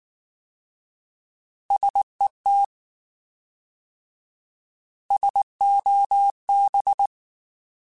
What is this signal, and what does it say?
Morse code, heard as a keyed tone, about 19 wpm, 800 Hz. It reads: SET SOB
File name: Alphabetical / Alphabetical mp3s / Ss @18